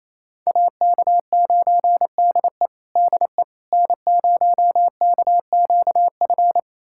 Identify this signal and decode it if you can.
Morse code, heard as a keyed tone, about 28 wpm, 700 Hz. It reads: AK9DE DE N0KQF
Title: USA Callsign DE Callsign - 5 Characters - 28wpm